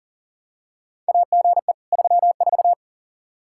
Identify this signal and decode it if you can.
Morse code, heard as a keyed tone, about 40 wpm, 700 Hz. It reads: AGE 34
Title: Practice QSO - Exchange 3 - More Info - 40wpm